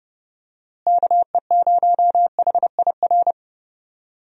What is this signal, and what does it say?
Morse code, heard as a keyed tone, about 31 wpm, 700 Hz. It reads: KE0HIR
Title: USA Call Signs - 6 Characters - Repeated 3x - 30wpm